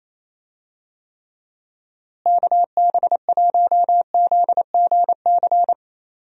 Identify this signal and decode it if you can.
Morse code, heard as a keyed tone, about 28 wpm, 700 Hz. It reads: KB1ZGC
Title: USA Call Signs - 6 Characters - Repeated 2x - 28wpm